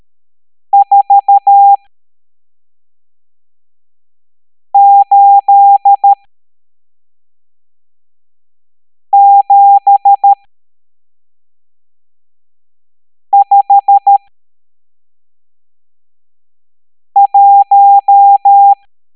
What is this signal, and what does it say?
Morse code, heard as a keyed tone, about 13 wpm, spaced as for 3 wpm, 800 Hz. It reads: 48751
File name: hamdata / L12-EX02-CHIFFRES